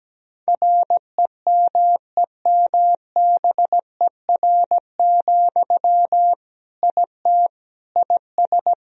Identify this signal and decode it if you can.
Morse code, heard as a keyed tone, about 17 wpm, 700 Hz. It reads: REMEMBER, IT IS
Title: Sentences from Top 1000 Words - Review Part 2 - 17wpm